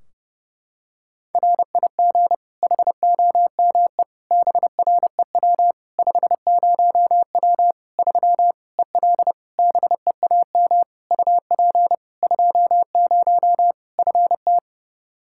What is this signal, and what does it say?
Morse code, heard as a keyed tone, about 30 wpm, 700 Hz. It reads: RIG HOME BREW 50W 3 EL BEAM UP 20 FT
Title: Practice QSO - Exchange 2 - Info - 30wpm